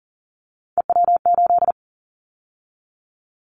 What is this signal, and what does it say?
Morse code, heard as a keyed tone, about 43 wpm, 700 Hz. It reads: EW8
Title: Callsign Prefixes for Top 50 DXCC - 40wpm